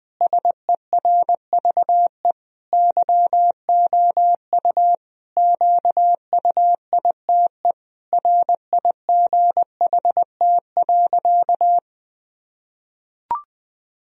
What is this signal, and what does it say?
Morse code, heard as a keyed tone, about 21 wpm, 700 Hz. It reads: SERVE YOU QUITE RIGHT.
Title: Sentences from Top 1000 Words - Review Part 2 - 20wpm